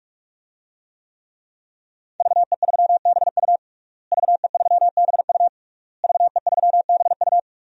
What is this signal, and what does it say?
Morse code, heard as a keyed tone, about 42 wpm, 700 Hz. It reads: VE3BU VE3BU VE3BU
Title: DXCC-100 Call Signs - 5 Characters - Repeated 3x - 45wpm